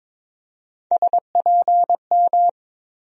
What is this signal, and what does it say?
Morse code, heard as a keyed tone, about 22 wpm, 700 Hz. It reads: SPM
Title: USA Call Sign Postfixes - 3 Letters 22wpm